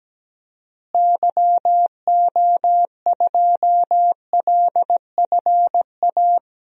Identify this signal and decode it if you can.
Morse code, heard as a keyed tone, about 17 wpm, 700 Hz. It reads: YO2LFA
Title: DXCC-100 Call Signs - 6 Characters - Repeated 2x - 17wpm